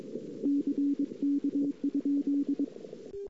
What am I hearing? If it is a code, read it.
Morse code, heard as a keyed tone, about 22 wpm, 280 Hz. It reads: CK?
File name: CW / 2008-ss-cw-k5tr-op-n5rz -06:00